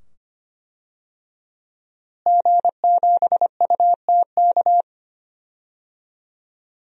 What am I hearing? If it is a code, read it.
Morse code, heard as a keyed tone, about 25 wpm, 700 Hz. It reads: G7UTK
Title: DXCC-100 Call Signs - 5 Characters - Repeated 2x - 25wpm